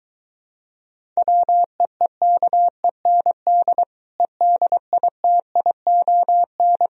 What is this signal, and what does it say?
Morse code, heard as a keyed tone, about 23 wpm, 700 Hz. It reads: WEEKEND EDITION